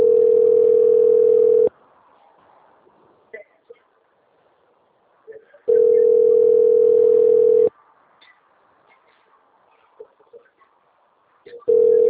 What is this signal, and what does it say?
Ringback tone (North America), 440 + 480 Hz.
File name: rg-1600-unknown- - - .20343